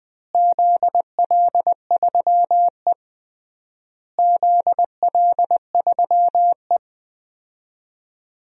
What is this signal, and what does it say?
Morse code, heard as a keyed tone, about 21 wpm, 700 Hz. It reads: ZL3E ZL3E
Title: DXCC-100 Call Signs - 4 Characters - Repeated 3x - 20wpm